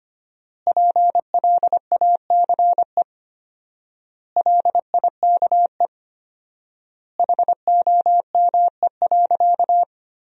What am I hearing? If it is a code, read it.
Morse code, heard as a keyed tone, about 25 wpm, 700 Hz. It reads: PLACE LIKE HOME.
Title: 4x Word-Spacing - Sentences from Top 200 Words - 25wpm